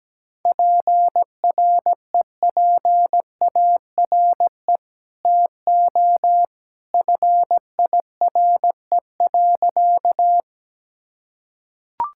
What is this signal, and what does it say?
Morse code, heard as a keyed tone, about 17 wpm, 700 Hz. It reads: PREPARE TO FIRE.
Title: Sentences from Top 1000 Words - Review Part 6 - 17wpm